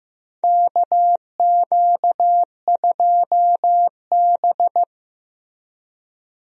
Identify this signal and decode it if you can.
Morse code, heard as a keyed tone, about 15 wpm, 700 Hz. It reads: KQ2B
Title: USA Call Signs - 4 Characters - 15wpm